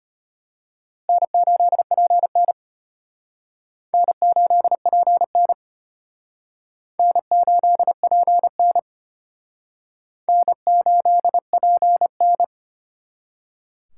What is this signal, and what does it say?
Morse code, heard as a keyed tone, about 32 wpm, 700 Hz. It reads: N8PN N8PN N8PN N8PN